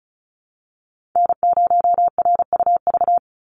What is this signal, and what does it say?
Morse code, heard as a keyed tone, about 36 wpm, 700 Hz. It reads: N0RUV
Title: USA Call Signs 35wpm